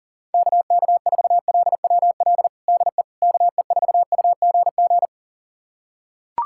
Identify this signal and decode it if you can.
Morse code, heard as a keyed tone, about 40 wpm, 700 Hz. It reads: KK4LWL DE KE4UGG
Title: USA Callsign DE Callsign - 6 Characters - 40wpm